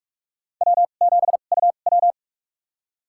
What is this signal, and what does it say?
Morse code, heard as a keyed tone, about 46 wpm, spaced as for 33 wpm, 700 Hz. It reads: W7UW
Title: USA Call Signs - 2x Character Spacing - 45wpm